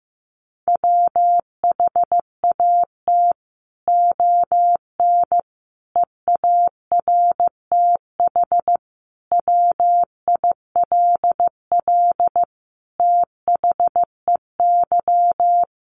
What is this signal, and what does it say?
Morse code, heard as a keyed tone, about 15 wpm, 700 Hz. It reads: WHAT ON EARTH WILL THEY
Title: Sentences from Top 1100 Words 15wpm